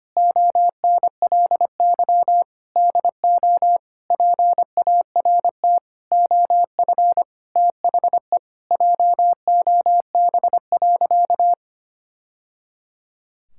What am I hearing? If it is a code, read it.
Morse code, heard as a keyed tone, about 25 wpm, 700 Hz. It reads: ONLY DO PART OF THE JOB.
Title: Sentences from Top 1000 Words - Review Part 5 - 25wpm